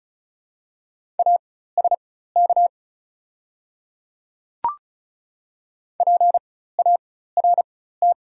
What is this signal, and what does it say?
Morse code, heard as a keyed tone, about 36 wpm, spaced as for 16 wpm, 700 Hz. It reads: ASK PART
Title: 4x Character Spacing - Top 200 Words - 35wpm